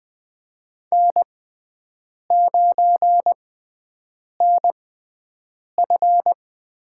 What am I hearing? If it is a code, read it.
Morse code, heard as a keyed tone, about 20 wpm, spaced as for 7 wpm, 700 Hz. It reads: N9NF
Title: USA Call Signs - 5 Characters - 6x Character Spacing - 20wpm